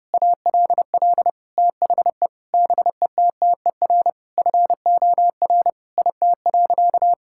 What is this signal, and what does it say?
Morse code, heard as a keyed tone, about 30 wpm, 700 Hz. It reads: ALL THE BETTER FOR IT.
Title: Sentences from Top 1000 Words - Review Part 5 - 30wpm